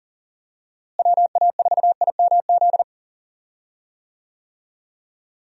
Morse code, heard as a keyed tone, about 40 wpm, 700 Hz. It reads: WA4IMZ
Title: USA Call Signs - 6 Characters - Repeated 3x - 40wpm